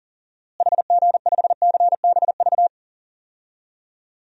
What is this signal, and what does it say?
Morse code, heard as a keyed tone, about 40 wpm, 700 Hz. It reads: HG5CBV
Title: DXCC-100 Call Signs - 6 Characters - Repeated 2x - 40wpm